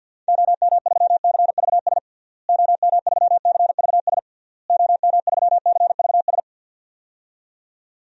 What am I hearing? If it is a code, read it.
Morse code, heard as a keyed tone, about 51 wpm, 700 Hz. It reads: OM3CVS OM3CVS OM3CVS
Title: DXCC-100 Call Signs - 6 Characters - Repeated 3x - 50wpm